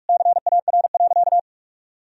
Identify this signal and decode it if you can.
Morse code, heard as a keyed tone, about 44 wpm, 700 Hz. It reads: CAR.
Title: Sentences from Top 1000 Words - Review Part 1 - 45wpm